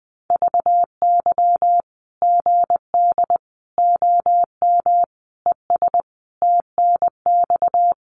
Morse code, heard as a keyed tone, about 20 wpm, 700 Hz. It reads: VY GD OM ES TNX
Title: Common QSO Phrases 20wpm